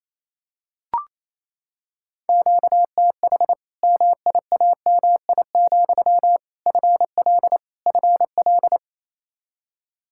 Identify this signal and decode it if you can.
Morse code, heard as a keyed tone, about 28 wpm, 700 Hz. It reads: QTH MIAMI, FL FL
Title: Practice QSO - Exchange 1 - Beginning - 28wpm